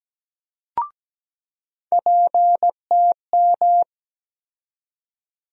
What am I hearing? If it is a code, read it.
Morse code, heard as a keyed tone, about 17 wpm, 700 Hz. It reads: PTM
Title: USA Call Sign Postfixes - 3 Letters 17wpm